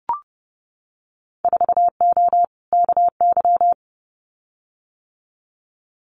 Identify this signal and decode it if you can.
Morse code, heard as a keyed tone, about 30 wpm, 700 Hz. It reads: 4O KY
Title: Winter Field Day Exchanges Encore 30wpm